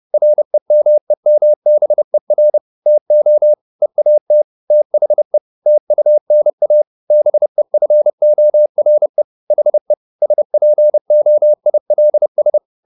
Morse code, heard as a keyed tone, about 30 wpm, 600 Hz. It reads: REMEMBER TO EAT THE TUNA BEFORE HE SPOILS
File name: the old man and the sea / the old man and the sea 1 30wpm